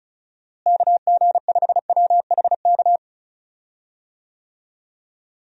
Morse code, heard as a keyed tone, about 36 wpm, 700 Hz. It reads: KG5WHK
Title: USA Call Signs - 6 Characters - Repeated 2x - 35wpm